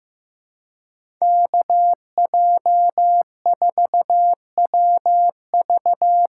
Morse code, heard as a keyed tone, about 15 wpm, 700 Hz. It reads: KJ4WV
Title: USA Call Signs - 6x Character Spacing - 15wpm